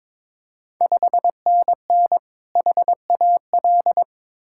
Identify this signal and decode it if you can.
Morse code, heard as a keyed tone, about 21 wpm, 700 Hz. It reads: 5NN HAL
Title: Ontario QSO Party Exchange 22wpm